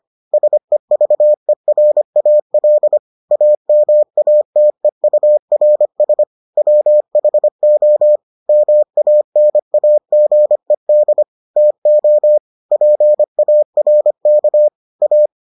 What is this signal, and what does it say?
Morse code, heard as a keyed tone, about 25 wpm, 600 Hz. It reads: SEVERAL AMATEURS WHO MANAGED TO PARK A